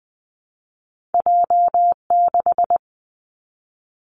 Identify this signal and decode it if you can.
Morse code, heard as a keyed tone, about 20 wpm, 700 Hz. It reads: J6
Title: Two Challenging Letter-Number Combinations 20wpm